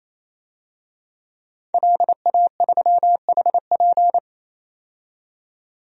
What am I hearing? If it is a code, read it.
Morse code, heard as a keyed tone, about 28 wpm, 700 Hz. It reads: LA3HP
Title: DXCC-100 Call Signs - 5 Characters - 28wpm